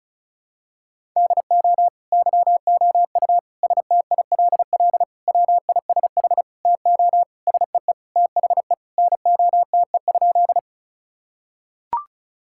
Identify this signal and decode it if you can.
Morse code, heard as a keyed tone, about 36 wpm, 700 Hz. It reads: DO YOU STILL WISH TO SEE THE NOTE?
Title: Sentences from Top 1600 Words 35wpm